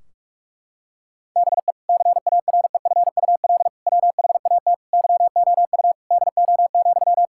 Morse code, heard as a keyed tone, about 46 wpm, 700 Hz. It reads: BE CAREFUL WHAT YOU DO,
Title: Sentences from Top 1600 Words 45wpm